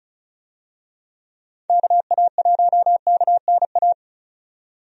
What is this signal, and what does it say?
Morse code, heard as a keyed tone, about 34 wpm, 700 Hz. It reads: KA1KNA
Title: USA Call Signs - 6 Characters - 35wpm